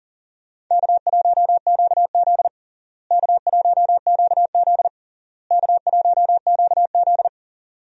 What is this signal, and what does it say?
Morse code, heard as a keyed tone, about 37 wpm, 700 Hz. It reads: K1QZ K1QZ K1QZ
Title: USA Call Signs - 4 Characters - Repeated 3x - 40wpm